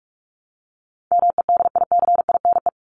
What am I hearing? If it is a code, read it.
Morse code, heard as a keyed tone, about 44 wpm, 700 Hz. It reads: MEDICINE
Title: Top 751-1000 Words 45wpm